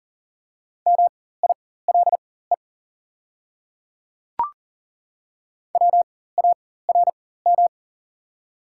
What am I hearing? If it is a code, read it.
Morse code, heard as a keyed tone, about 40 wpm, spaced as for 19 wpm, 700 Hz. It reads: MILE WARM
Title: 4x Character Spacing - Top 500 Words - 40wpm